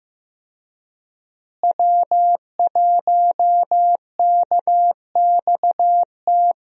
Morse code, heard as a keyed tone, about 16 wpm, 700 Hz. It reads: W1KXT
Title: USA Call Signs - 5 Characters - Repeated 2x - 15wpm